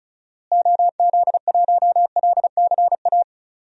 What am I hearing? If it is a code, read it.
Morse code, heard as a keyed tone, about 36 wpm, 700 Hz. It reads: OZ1LCA
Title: DXCC-100 Call Signs - 6 Characters - Repeated 2x - 35wpm